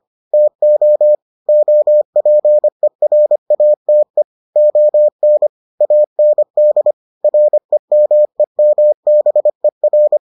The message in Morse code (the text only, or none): TO OPERATE ON AND REMEMBER